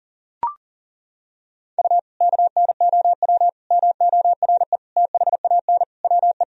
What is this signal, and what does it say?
Morse code, heard as a keyed tone, about 40 wpm, 700 Hz. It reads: U KNOW MORE THAN WE
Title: Sentences from Top 100 Words and Abbreviations 40wpm